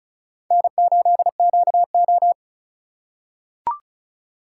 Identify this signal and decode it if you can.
Morse code, heard as a keyed tone, about 34 wpm, 700 Hz. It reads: N8QO
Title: USA Call Signs - 4 Characters - 4x Character Spacing - 35wpm